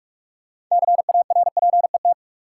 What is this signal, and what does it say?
Morse code, heard as a keyed tone, about 47 wpm, 700 Hz. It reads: CARPET
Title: Six Letter Words - 45wpm